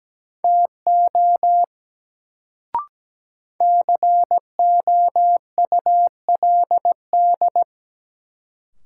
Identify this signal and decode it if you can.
Morse code, heard as a keyed tone, about 17 wpm, 700 Hz. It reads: TO COULD
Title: Sentences from Top 100 Words - New Word Drill - 17wpm